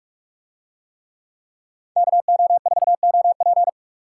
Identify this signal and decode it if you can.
Morse code, heard as a keyed tone, about 44 wpm, 700 Hz. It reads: KO4OP